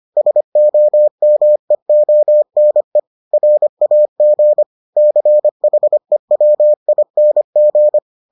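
Morse code, heard as a keyed tone, about 25 wpm, 600 Hz. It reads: SOMEONE RAG CHEWING
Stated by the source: Morse edition of the Foundations of Amateur Radio episodes.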